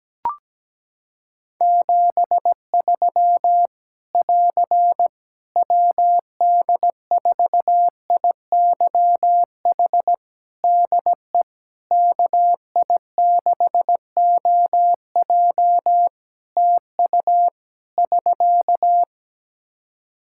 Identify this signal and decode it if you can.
Morse code, heard as a keyed tone, about 17 wpm, 700 Hz. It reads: 73 <AR> WD4IYH DE KI6OJ TU <SK>
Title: K1USN SST QSO Practice 17wpm